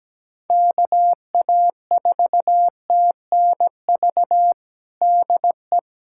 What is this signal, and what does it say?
Morse code, heard as a keyed tone, about 17 wpm, 700 Hz. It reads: KA4TNV DE
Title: USA Callsign DE Callsign - 6 Characters - 17wpm